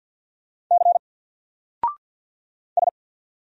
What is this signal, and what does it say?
Morse code, heard as a keyed tone, about 50 wpm, 700 Hz. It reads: C S
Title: Single Letter-Number 50wpm